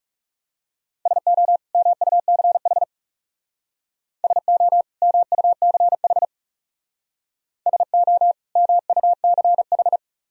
Morse code, heard as a keyed tone, about 37 wpm, 700 Hz. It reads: SO MUCH SO MUCH SO MUCH
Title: Speed-Racing - Sets of 2 Words 30wpm